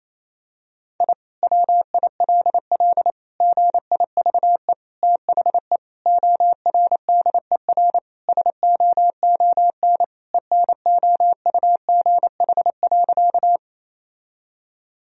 Morse code, heard as a keyed tone, about 27 wpm, 700 Hz. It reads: I WILL GIVE THE ORDER SOON ENOUGH.
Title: Sentences from Top 1000 Words - Review Part 2 - 28wpm